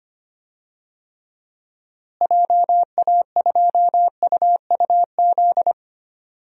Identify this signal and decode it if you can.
Morse code, heard as a keyed tone, about 25 wpm, 700 Hz. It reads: JA2UUZ
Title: DXCC-100 Call Signs - 6 Characters - Repeated 3x - 25wpm